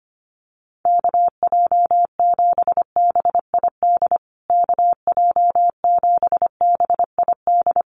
Morse code, heard as a keyed tone, about 25 wpm, 700 Hz. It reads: KJ7BID KJ7BID
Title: USA Call Signs Repeated 3x 25wpm